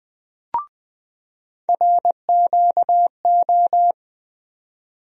Morse code, heard as a keyed tone, about 20 wpm, 700 Hz. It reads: RQO
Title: USA Call Sign Postfixes - 3 Letters 20wpm